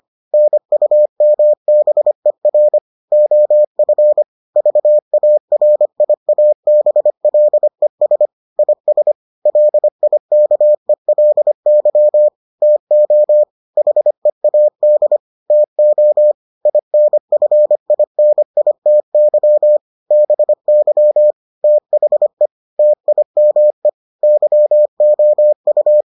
Morse code, heard as a keyed tone, about 25 wpm, 600 Hz. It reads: NUMBER OF VARIABLES IS LIKELY TO HEAD TO INFINITY BY THE TIME YOU